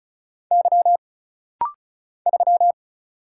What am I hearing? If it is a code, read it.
Morse code, heard as a keyed tone, about 36 wpm, 700 Hz. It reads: Y 3
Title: Single Letter-Number - Rapid-Fire 35wpm